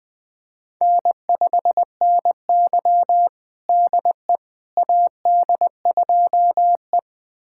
Morse code, heard as a keyed tone, about 20 wpm, 700 Hz. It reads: N5NY DE AD2E
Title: USA Callsign DE Callsign - 4 Characters - 20wpm